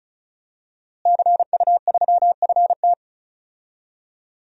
Morse code, heard as a keyed tone, about 36 wpm, 700 Hz. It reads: CU3FT
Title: DXCC-100 Call Signs - 5 Characters - Repeated 3x - 35wpm